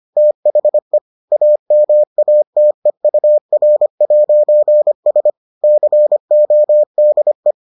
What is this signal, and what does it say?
Morse code, heard as a keyed tone, about 25 wpm, 600 Hz. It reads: THE AMATEUR'S CODE